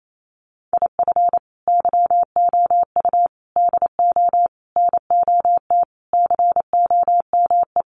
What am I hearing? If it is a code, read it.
Morse code, heard as a keyed tone, about 28 wpm, 700 Hz. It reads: IF YOU DO NOT COME
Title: Sentences from Top 800 Words 28wpm